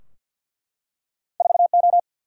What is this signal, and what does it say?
Morse code, heard as a keyed tone, about 50 wpm, 700 Hz. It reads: 4 O
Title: Single-Letter Single-Number 50wpm